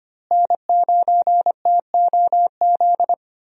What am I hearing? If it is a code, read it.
Morse code, heard as a keyed tone, about 25 wpm, 700 Hz. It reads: N9TOZ
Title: USA Call Signs - 5 Characters - Repeated 3x - 25wpm